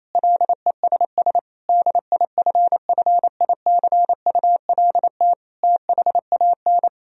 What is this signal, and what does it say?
Morse code, heard as a keyed tone, about 27 wpm, 700 Hz. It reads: LESS DIFFICULT THAN
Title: Sentences from Top 1000 Words - Review Part 2 - 28wpm